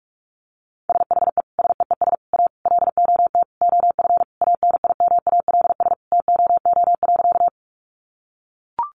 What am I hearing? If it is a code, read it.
Morse code, heard as a keyed tone, about 44 wpm, 700 Hz. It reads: SHE SEES A LOT OF ANIMALS TOO.
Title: Easy Phrases 45wpm